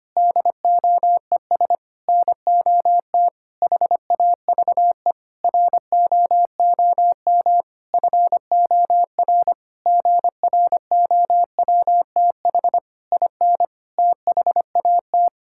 Morse code, heard as a keyed tone, about 25 wpm, 700 Hz. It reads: DOES NOT HAVE ROOM FOR GROWTH IN THAT
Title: Sentences from Top 1000 Words - Review Part 5 - 25wpm